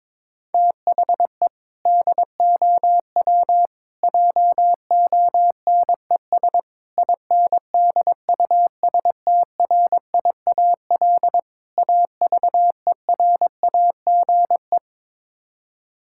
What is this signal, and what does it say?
Morse code, heard as a keyed tone, about 22 wpm, 700 Hz. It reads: THE DOW JONES INDUSTRIAL AVERAGE
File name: practice / 22wpm / Build A Phrase - 5-Word Phrases - 22wpm